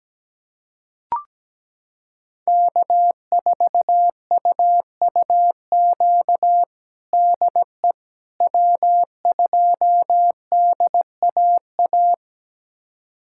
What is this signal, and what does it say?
Morse code, heard as a keyed tone, about 17 wpm, 700 Hz. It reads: K4UUQ DE W2DAA
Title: USA Callsign DE Callsign - 5 Characters - 17wpm